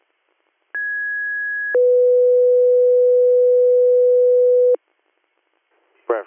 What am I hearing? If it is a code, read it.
Two-tone page, heard as a one-second tone, then a three-second tone, about 1650 then 500 Hz, with speech. It is heard from 0.7 to 4.7 s.